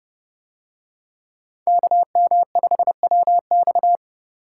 Morse code, heard as a keyed tone, about 30 wpm, 700 Hz. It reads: KM5WX